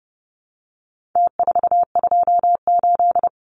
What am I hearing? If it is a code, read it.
Morse code, heard as a keyed tone, about 30 wpm, 700 Hz. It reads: T428
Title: Numbers - Contest Style - 4-Digit 30wpm